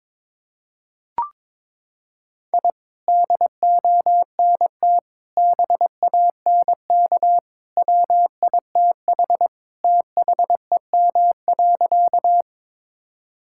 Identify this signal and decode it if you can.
Morse code, heard as a keyed tone, about 22 wpm, 700 Hz. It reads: I DONT BANK WITH THEM.
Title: Sentences from Top 1000 Words - Review Part 6 - 22wpm